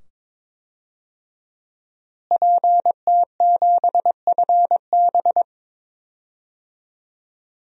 Morse code, heard as a keyed tone, about 22 wpm, 700 Hz. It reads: PT7FB
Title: DXCC-100 Call Signs - 5 Characters - Repeated 2x - 22wpm